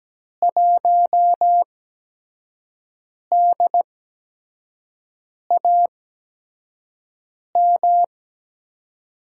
Morse code, heard as a keyed tone, about 17 wpm, spaced as for 5 wpm, 700 Hz. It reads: 1DAM
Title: USA Call Signs - 6 Characters - 8x Character Spacing - 17wpm